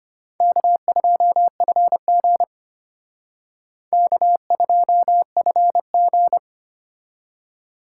Morse code, heard as a keyed tone, about 32 wpm, 700 Hz. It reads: K2FG K2FG